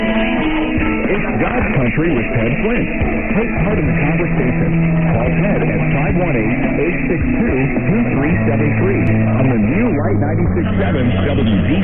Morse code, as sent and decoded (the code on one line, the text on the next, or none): none